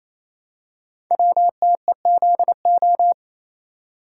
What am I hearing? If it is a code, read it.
Morse code, heard as a keyed tone, about 28 wpm, 700 Hz. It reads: WTEZO
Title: Code Groups - 5 Letters - 28wpm